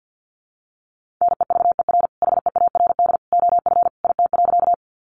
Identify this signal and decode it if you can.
Morse code, heard as a keyed tone, about 50 wpm, 700 Hz. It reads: NEVER HEARD OF IT.
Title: Sentences from Top 400 Words 50wpm